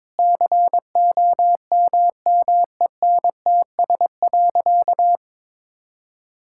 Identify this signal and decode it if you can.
Morse code, heard as a keyed tone, about 22 wpm, 700 Hz. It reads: COMMENTS.
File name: practice / 22wpm / Sentences with 8-Word Phrases - 22wpm